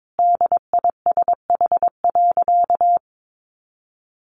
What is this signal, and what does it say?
Morse code, heard as a keyed tone, about 22 wpm, 700 Hz. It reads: DISH.
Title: Sentences from Top 1300 Words 22wpm